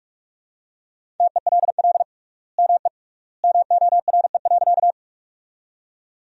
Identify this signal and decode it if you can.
Morse code, heard as a keyed tone, about 44 wpm, 700 Hz. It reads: TELL ME MORE.
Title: 3x Word-Spacing - Sentences from Top 300 Words - 45wpm